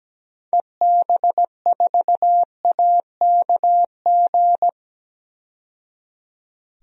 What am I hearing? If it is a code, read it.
Morse code, heard as a keyed tone, about 17 wpm, 700 Hz. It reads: EB4AKG
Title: DXCC-100 Call Signs - 6 Characters - 17wpm